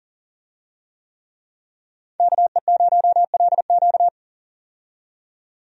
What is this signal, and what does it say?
Morse code, heard as a keyed tone, about 40 wpm, 700 Hz. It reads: KE0LQ
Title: USA Call Signs - 5 Characters - 40wpm